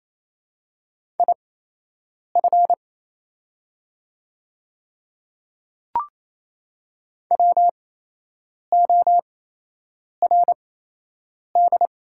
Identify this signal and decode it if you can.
Morse code, heard as a keyed tone, about 28 wpm, spaced as for 8 wpm, 700 Hz. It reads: IF WORD
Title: 8x Character Spacing - Top 200 Words - 28wpm